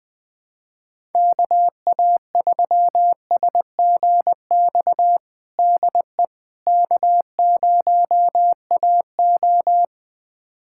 Morse code, heard as a keyed tone, about 20 wpm, 700 Hz. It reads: KA3SGX DE K0AO
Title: USA Callsign DE Callsign 20wpm